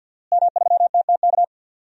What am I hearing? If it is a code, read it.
Morse code, heard as a keyed tone, about 47 wpm, 700 Hz. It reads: M3TTK
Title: DXCC-100 Call Signs - 5 Characters - Repeated 2x - 50wpm